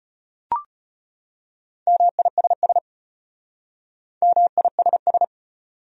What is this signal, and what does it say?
Morse code, heard as a keyed tone, about 33 wpm, 700 Hz. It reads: MISS MISS